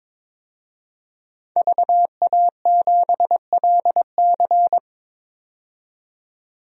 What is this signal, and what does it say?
Morse code, heard as a keyed tone, about 22 wpm, 700 Hz. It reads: VA7LC
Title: DXCC-100 Call Signs - 5 Characters - 22wpm